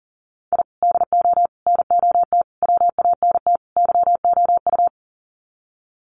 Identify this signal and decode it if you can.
Morse code, heard as a keyed tone, about 38 wpm, 700 Hz. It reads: I DO NOT WANT YOU
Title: Sets of 5 Words 40wpm